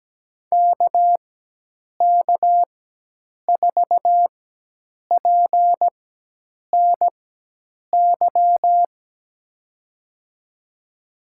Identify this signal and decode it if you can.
Morse code, heard as a keyed tone, about 17 wpm, spaced as for 8 wpm, 700 Hz. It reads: KK4PNY
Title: USA Call Signs - 6 Characters - 4x Character Spacing - 17wpm